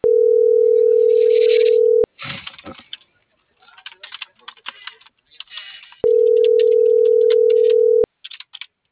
Ringback tone (North America), 440 + 480 Hz.